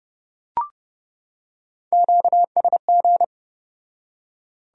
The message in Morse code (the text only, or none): QSG